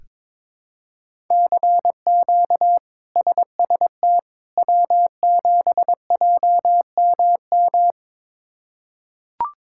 Morse code, heard as a keyed tone, about 22 wpm, 700 Hz. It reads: CQ SST W7JMM
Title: K1USN SST QSO Practice - 22wpm